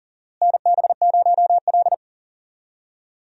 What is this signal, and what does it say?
Morse code, heard as a keyed tone, about 40 wpm, 700 Hz. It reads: NB0L
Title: USA Call Signs - 4 Characters - Repeated 2x - 40wpm